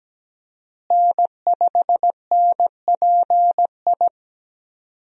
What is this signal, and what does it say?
Morse code, heard as a keyed tone, about 17 wpm, 700 Hz. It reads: N5NPI
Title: USA Call Signs - 5 Characters - Repeated 2x - 17wpm